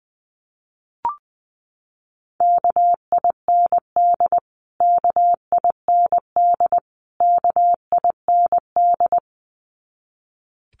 Morse code, heard as a keyed tone, about 20 wpm, 700 Hz. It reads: KIND KIND KIND
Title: Top 500 Words Repeated 20wpm